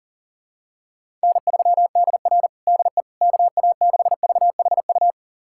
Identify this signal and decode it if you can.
Morse code, heard as a keyed tone, about 40 wpm, 700 Hz. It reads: N3DR DE KA6VHU
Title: USA Callsign DE Callsign 40wpm